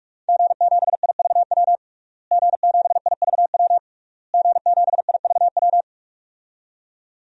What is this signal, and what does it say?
Morse code, heard as a keyed tone, about 41 wpm, 700 Hz. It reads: G7IVW G7IVW G7IVW
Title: DXCC-100 Call Signs - 5 Characters - Repeated 3x - 45wpm